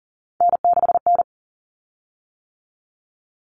Morse code, heard as a keyed tone, about 40 wpm, 700 Hz. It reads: N6N
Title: Common Cut Numbers - 3-Digit 40wpm